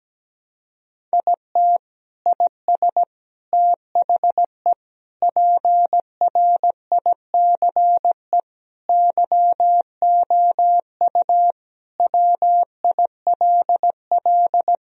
Morse code, heard as a keyed tone, about 17 wpm, 700 Hz. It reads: IT IS THE PRICE YOU WILL
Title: Sentences from Top 1000 Words - Review Part 4 - 17wpm